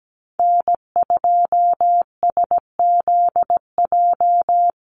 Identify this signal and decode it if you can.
Morse code, heard as a keyed tone, about 17 wpm, 700 Hz. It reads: N2SZJ
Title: USA Call Signs Repeated 2x 17wpm